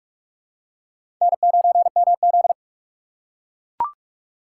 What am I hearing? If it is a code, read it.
Morse code, heard as a keyed tone, about 42 wpm, 700 Hz. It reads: N9MZ
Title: USA Call Signs - 2x Character Spacing - 45wpm